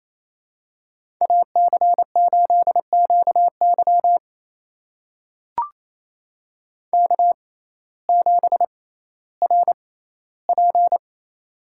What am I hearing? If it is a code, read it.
Morse code, heard as a keyed tone, about 27 wpm, spaced as for 15 wpm, 700 Hz. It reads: AC8QY K7RP
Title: USA Call Signs - 5 Characters - 6x Character Spacing - 28wpm